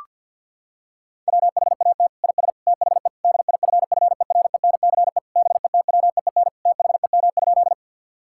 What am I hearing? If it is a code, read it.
Morse code, heard as a keyed tone, about 50 wpm, 700 Hz. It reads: WHAT IS THE DIFFERENCE BETWEEN THEM?
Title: Sentences from Top 1000 Words - Review Part 5 - 50wpm